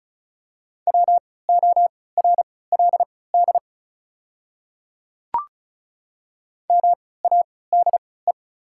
Morse code, heard as a keyed tone, about 34 wpm, spaced as for 20 wpm, 700 Hz. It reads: WORLD MADE
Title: 3x Character Spacing - Top 200 Words - 35wpm